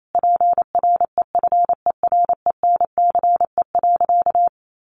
Morse code, heard as a keyed tone, about 28 wpm, 700 Hz. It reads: PREFERENCE.
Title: Sentences from Top 1500 Words 28wpm